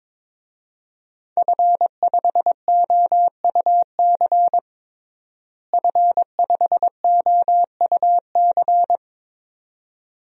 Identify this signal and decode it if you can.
Morse code, heard as a keyed tone, about 22 wpm, 700 Hz. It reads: F5OUC F5OUC
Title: DXCC-100 Call Signs - 5 Characters - Repeated 2x - 22wpm